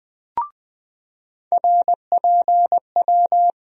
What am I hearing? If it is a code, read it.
Morse code, heard as a keyed tone, about 21 wpm, 700 Hz. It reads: RPW
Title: USA Call Sign Postfixes - 3 Letters 20wpm